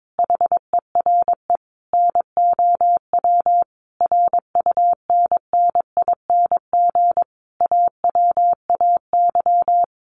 Morse code, heard as a keyed tone, about 22 wpm, 700 Hz. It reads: HERE NOW RUNNING AWAY
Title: A Sound Of Thunder 22wpm